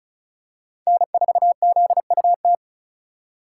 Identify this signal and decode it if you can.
Morse code, heard as a keyed tone, about 34 wpm, 700 Hz. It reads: N4ZUT
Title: USA Call Signs - 5 Characters - 3x Character Spacing - 35wpm